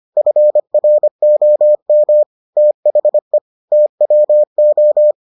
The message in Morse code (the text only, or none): FROM THE TWO